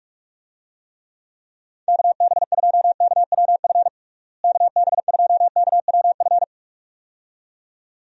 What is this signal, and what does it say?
Morse code, heard as a keyed tone, about 41 wpm, 700 Hz. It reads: KB2KWF KB2KWF
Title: USA Call Signs - 6 Characters - Repeated 2x - 45wpm